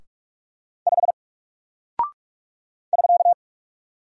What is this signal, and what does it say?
Morse code, heard as a keyed tone, about 44 wpm, 700 Hz. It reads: 5 <SK>
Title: Single Character (Letter:Number:Punctuation:Prosign) Alt 45wpm